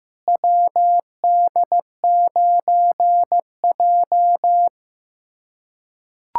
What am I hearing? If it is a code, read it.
Morse code, heard as a keyed tone, about 15 wpm, 700 Hz. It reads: WD9J
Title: USA Callsign DE Callsign - 4 Characters - 15wpm